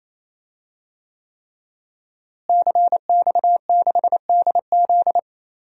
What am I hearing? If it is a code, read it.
Morse code, heard as a keyed tone, about 28 wpm, 700 Hz. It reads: CX6DZ
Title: DXCC-100 Call Signs - 5 Characters - Repeated 3x - 28wpm